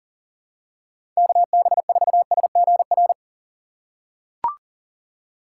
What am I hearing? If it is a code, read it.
Morse code, heard as a keyed tone, about 40 wpm, 700 Hz. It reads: KB4SGR
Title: USA Call Signs - 3x Character Spacing - 40wpm